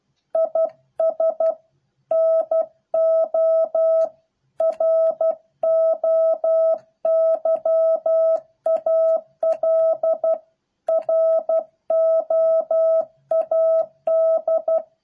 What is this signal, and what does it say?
Morse code, heard as a keyed tone, about 12 wpm, 650 Hz. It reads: IS NO ROYAL ROAD